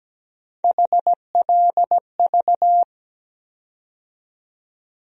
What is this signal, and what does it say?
Morse code, heard as a keyed tone, about 17 wpm, 700 Hz. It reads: HLV
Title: USA Call Sign Postfixes - 3 Letters 17wpm